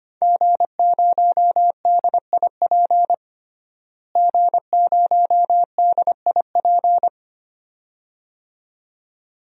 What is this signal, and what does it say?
Morse code, heard as a keyed tone, about 25 wpm, 700 Hz. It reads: G0DIP G0DIP
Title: DXCC-100 Call Signs - 5 Characters - Repeated 2x - 25wpm